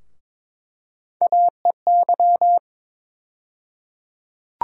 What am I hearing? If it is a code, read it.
Morse code, heard as a keyed tone, about 22 wpm, 700 Hz. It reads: AEY
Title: USA Call Sign Postfixes - 3 Letters 22wpm